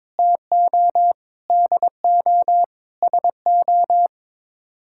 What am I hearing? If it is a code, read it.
Morse code, heard as a keyed tone, about 22 wpm, 700 Hz. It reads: TO DO SO
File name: practice / 22wpm / Sets of 6 Words - 22wpm